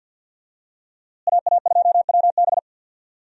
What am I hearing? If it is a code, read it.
Morse code, heard as a keyed tone, about 46 wpm, 700 Hz. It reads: AA2WB
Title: USA Call Signs - 5 Characters - Repeated 3x - 50wpm